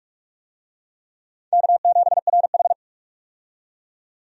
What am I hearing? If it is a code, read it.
Morse code, heard as a keyed tone, about 44 wpm, 700 Hz. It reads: K7RH
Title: USA Call Signs - 4 Characters - 45wpm